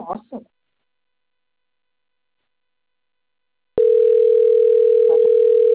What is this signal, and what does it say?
Ringback tone (North America), 440 + 480 Hz, with speech.
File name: rg-1600-unknown- - - .19046